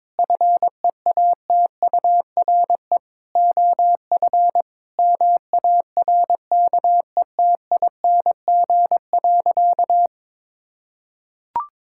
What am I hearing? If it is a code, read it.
Morse code, heard as a keyed tone, about 22 wpm, 700 Hz. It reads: FEATURE OF MARKETING.
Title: Sentences from Top 1000 Words - Review Part 6 - 22wpm